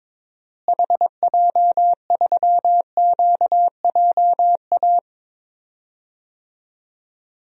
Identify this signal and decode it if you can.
Morse code, heard as a keyed tone, about 22 wpm, 700 Hz. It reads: HJ3QJA
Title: DXCC-100 Call Signs - 6 Characters - Repeated 2x - 22wpm